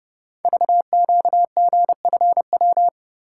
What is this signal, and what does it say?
Morse code, heard as a keyed tone, about 29 wpm, 700 Hz. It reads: VQGFW
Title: Code Groups - 5 Letters - 30wpm